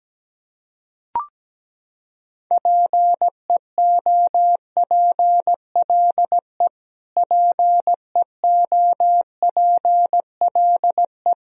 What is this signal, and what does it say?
Morse code, heard as a keyed tone, about 17 wpm, 700 Hz. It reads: PEOPLE PEOPLE
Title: Top 100 Words Repeated 17wpm